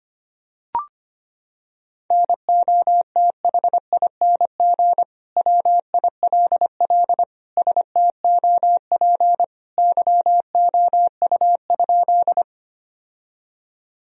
Morse code, heard as a keyed tone, about 25 wpm, 700 Hz. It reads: NOTHING WILL STOP YOU?
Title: Sentences from Top 1000 Words - Review Part 1 - 25wpm